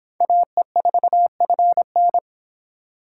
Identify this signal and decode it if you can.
Morse code, heard as a keyed tone, about 26 wpm, 700 Hz. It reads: AE4FN
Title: Speed Racing - USA 2x2 Call Signs 22wpm